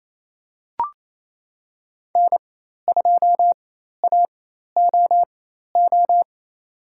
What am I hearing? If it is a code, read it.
Morse code, heard as a keyed tone, about 28 wpm, spaced as for 13 wpm, 700 Hz. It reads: N2AOO
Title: USA Call Signs - 5 Characters - 4x Character Spacing - 28wpm